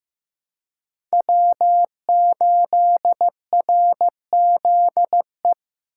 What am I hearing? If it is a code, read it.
Morse code, heard as a keyed tone, about 15 wpm, 700 Hz. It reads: W8RZE
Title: USA Call Signs - 2x Character Spacing - 15wpm